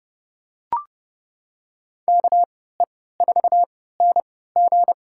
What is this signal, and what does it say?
Morse code, heard as a keyed tone, about 30 wpm, spaced as for 17 wpm, 700 Hz. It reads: KE4NG
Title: USA Call Signs - 3x Character Spacing - 30wpm